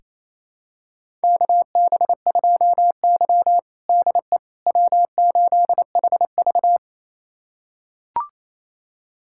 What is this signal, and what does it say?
Morse code, heard as a keyed tone, about 28 wpm, 700 Hz. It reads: KB2Y DE W8HV
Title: USA Callsign DE Callsign - 4 Characters - 28wpm